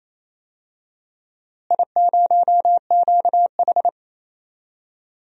Morse code, heard as a keyed tone, about 28 wpm, 700 Hz. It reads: I0QH
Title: DXCC-100 Call Signs - 4 Characters - Repeated 3x - 28wpm